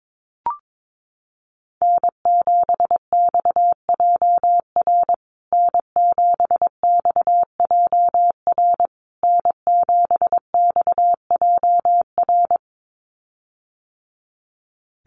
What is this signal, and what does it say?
Morse code, heard as a keyed tone, about 22 wpm, 700 Hz. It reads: N7XJR N7XJR N7XJR
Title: USA Call Signs Repeated 3x 22wpm